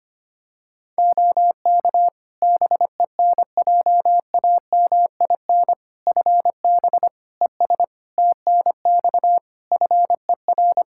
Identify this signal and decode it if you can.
Morse code, heard as a keyed tone, about 25 wpm, 700 Hz. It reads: OK BENJAMIN FB ES TNX FER
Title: K1USN SST QSO Practice 25wpm